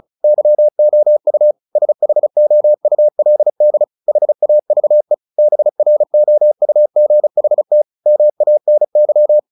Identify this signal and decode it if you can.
Morse code, heard as a keyed tone, about 35 wpm, 600 Hz. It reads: YOU SHOULD HAVE BROUGHT MANY
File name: the old man and the sea 2 35wpm